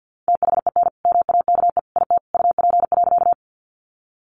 Morse code, heard as a keyed tone, about 50 wpm, 700 Hz. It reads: THEN MAKE IT UP.
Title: Sentences from Top 100 Words 50wpm